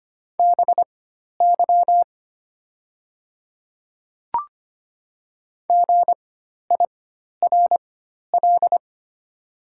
Morse code, heard as a keyed tone, about 25 wpm, spaced as for 12 wpm, 700 Hz. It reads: BY GIRL
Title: 4x Character Spacing - Top 500 Words - 25wpm